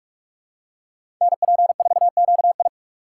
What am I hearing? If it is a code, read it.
Morse code, heard as a keyed tone, about 47 wpm, 700 Hz. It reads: NP4QI
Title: USA Call Signs - 5 Characters - 2x Character Spacing - 45wpm